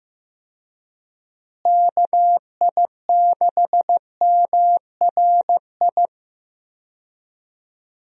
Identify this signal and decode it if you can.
Morse code, heard as a keyed tone, about 15 wpm, 700 Hz. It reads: KI6MRI
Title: USA Call Signs - 6 Characters - Repeated 3x - 15wpm